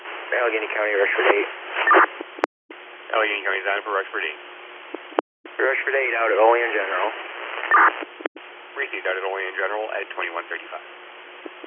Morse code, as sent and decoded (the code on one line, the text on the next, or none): none